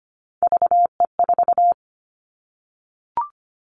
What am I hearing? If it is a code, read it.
Morse code, heard as a keyed tone, about 25 wpm, 700 Hz. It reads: VE4
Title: Callsign Prefixes for Top 25 DXCC - Repeated 3x - 25wpm